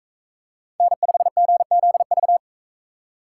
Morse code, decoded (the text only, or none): N5GZV